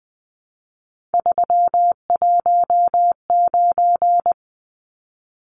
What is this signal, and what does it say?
Morse code, heard as a keyed tone, about 20 wpm, 700 Hz. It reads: 319
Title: Numbers - 3-Digit 20wpm